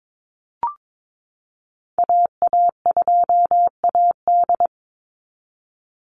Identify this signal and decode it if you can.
Morse code, heard as a keyed tone, about 22 wpm, 700 Hz. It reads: AA2AD
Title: Call Signs Repeated 2x - 22wpm